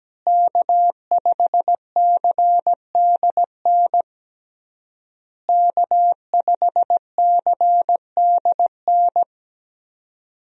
Morse code, heard as a keyed tone, about 17 wpm, 700 Hz. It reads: K5CDN K5CDN
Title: USA Call Signs - 5 Characters - Repeated 3x - 17wpm